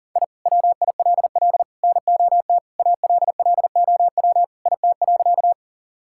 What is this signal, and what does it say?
Morse code, heard as a keyed tone, about 40 wpm, 700 Hz. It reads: I WILL NOT ALLOW IT.
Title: Sentences from Top 1000 Words - Review Part 5 - 40wpm